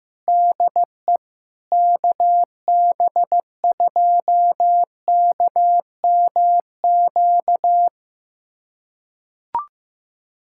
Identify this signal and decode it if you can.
Morse code, heard as a keyed tone, about 15 wpm, 700 Hz. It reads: DE KB2KMQ
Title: USA Callsign DE Callsign - 6 Characters - 15wpm